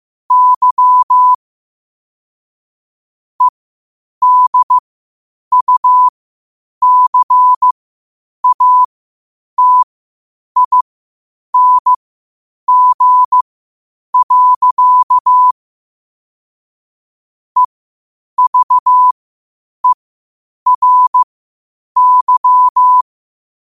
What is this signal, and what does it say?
Morse code, heard as a keyed tone, about 15 wpm, spaced as for 8 wpm, 1000 Hz. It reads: Y EDUCATING. EVERY
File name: qotdfiles / 8wpm